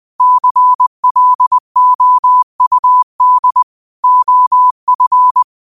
Morse code, heard as a keyed tone, about 20 wpm, 1000 Hz. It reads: CLOUD OF
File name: qotdfiles / 20wpm